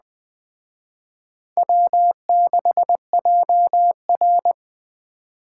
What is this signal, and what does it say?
Morse code, heard as a keyed tone, about 23 wpm, 700 Hz. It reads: W6JR
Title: Speed Racing - USA 1x2 Call Signs 20wpm